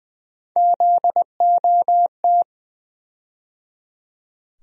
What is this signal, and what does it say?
Morse code, heard as a keyed tone, about 20 wpm, 700 Hz. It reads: ZOT
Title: USA Call Sign Postfixes - 3 Letters 20wpm